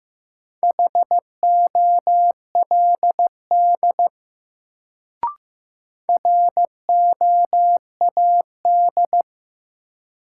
Morse code, heard as a keyed tone, about 15 wpm, 700 Hz. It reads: HOLD ROAD
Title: Sentences from Top 400 Words - New Word Drill - 15wpm